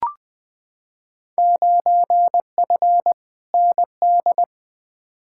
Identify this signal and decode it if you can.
Morse code, heard as a keyed tone, about 21 wpm, 700 Hz. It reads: 9F ND
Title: Field Day Exchanges Encore - 20wpm